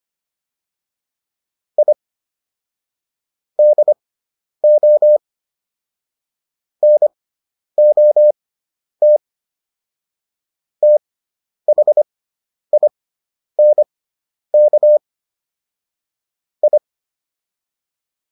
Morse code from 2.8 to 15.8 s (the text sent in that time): DO NOT THINK